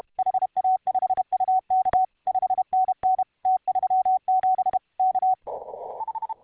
Morse code, heard as a keyed tone, about 32 wpm, 750 Hz. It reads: HA5UK 5NN T37 K